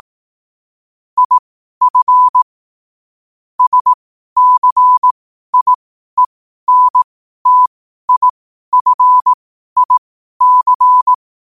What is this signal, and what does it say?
Morse code, heard as a keyed tone, about 18 wpm, spaced as for 12 wpm, 1000 Hz. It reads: IF SCIENTIFIC
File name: qotdfiles / 12wpm